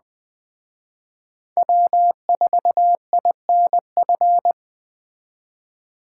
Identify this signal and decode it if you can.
Morse code, heard as a keyed tone, about 20 wpm, 700 Hz. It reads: W4INF